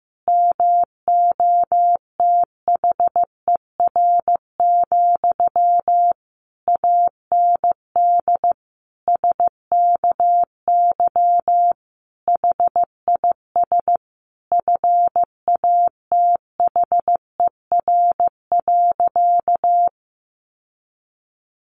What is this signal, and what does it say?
Morse code, heard as a keyed tone, about 15 wpm, 700 Hz. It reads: MOTHER, AND SKY HIS FATHER.
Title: Sentences from Top 1200 Words 15wpm